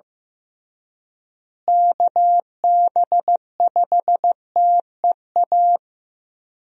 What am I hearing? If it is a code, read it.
Morse code, heard as a keyed tone, about 15 wpm, 700 Hz. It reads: KB5TEA
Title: USA Call Signs - 6 Characters - Repeated 3x - 15wpm